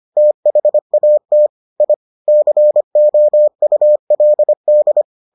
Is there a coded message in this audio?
Yes: Morse code from 0.2 to 5.0 s, heard as a keyed tone, about 25 wpm, 600 Hz. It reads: THAT I COULD